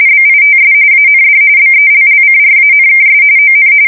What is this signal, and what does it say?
RTTY, about 45.45 baud. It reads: BITS AND BELLCRANKS ON